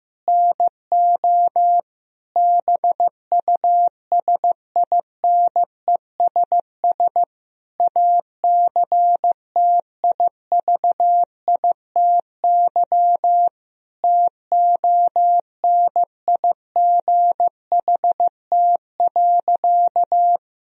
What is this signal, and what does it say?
Morse code, heard as a keyed tone, about 15 wpm, 700 Hz. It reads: NO BUSINESS ACTIVITY TONIGHT.
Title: Sentences from Top 1000 Words - Review Part 4 - 15wpm